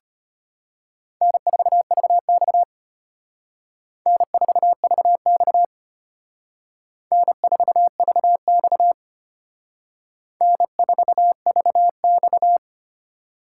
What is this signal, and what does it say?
Morse code, heard as a keyed tone, about 32 wpm, 700 Hz. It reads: N4VX N4VX N4VX N4VX